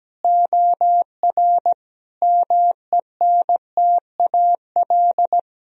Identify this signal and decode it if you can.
Morse code, heard as a keyed tone, about 17 wpm, 700 Hz. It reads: OR MENTAL
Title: Sentences from Top 1000 Words - Review Part 4 - 17wpm